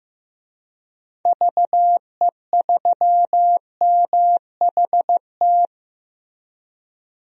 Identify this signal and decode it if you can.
Morse code, heard as a keyed tone, about 15 wpm, 700 Hz. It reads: VE3MHT
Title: DXCC-100 Call Signs - 6 Characters - Repeated 3x - 15wpm